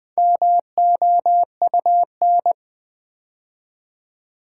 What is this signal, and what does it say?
Morse code, heard as a keyed tone, about 20 wpm, 700 Hz. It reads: MOUN
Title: Build A Word - 4 Letters At A Time - 8-Letter Words - 20wpm